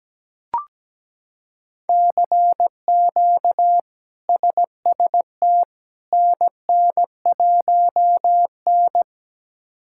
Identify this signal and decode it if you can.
Morse code, heard as a keyed tone, about 17 wpm, 700 Hz. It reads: CQ SST NN1N
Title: K1USN SST QSO Practice - 17wpm